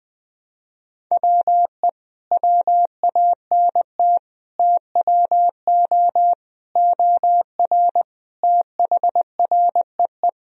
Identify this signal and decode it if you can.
Morse code, heard as a keyed tone, about 20 wpm, 700 Hz. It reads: WE WANT TWO OR THREE
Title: Sentences from Top 1000 Words - Review Part 1 - 20wpm